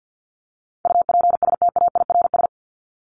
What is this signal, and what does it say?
Morse code, heard as a keyed tone, about 50 wpm, 700 Hz. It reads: UPSTAIRS
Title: Top 1751-2000 Words 50wpm